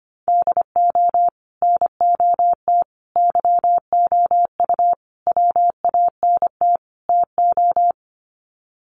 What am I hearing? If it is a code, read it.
Morse code, heard as a keyed tone, about 25 wpm, 700 Hz. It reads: DO NOT YOU WANT TO
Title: Sets of 5 Words Encore2 25wpm